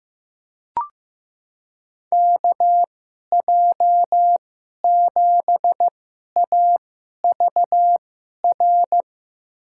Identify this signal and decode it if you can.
Morse code, heard as a keyed tone, about 15 wpm, spaced as for 11 wpm, 700 Hz. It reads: KJ7AVR
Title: USA Call Signs - 2x Character Spacing - 15wpm